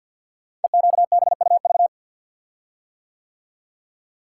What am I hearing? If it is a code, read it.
Morse code, heard as a keyed tone, about 46 wpm, 700 Hz. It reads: EQBUV
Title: Code Groups - 5 Letters - 50wpm